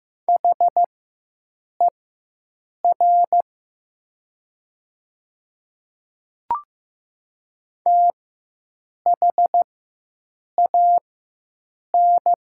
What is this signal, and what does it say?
Morse code, heard as a keyed tone, about 15 wpm, spaced as for 7 wpm, 700 Hz. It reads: HER THAN
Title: 4x Character Spacing - Top 100 Words - 15wpm